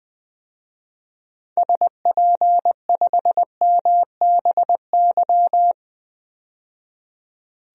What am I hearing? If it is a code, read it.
Morse code, heard as a keyed tone, about 20 wpm, 700 Hz. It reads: SP5MBY
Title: DXCC-100 Call Signs - 6 Characters - 20wpm